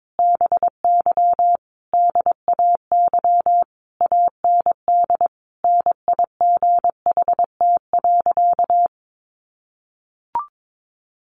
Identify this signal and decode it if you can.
Morse code, heard as a keyed tone, about 22 wpm, 700 Hz. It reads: BY DAY AND NIGHT.
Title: Sentences from Top 1500 Words 22wpm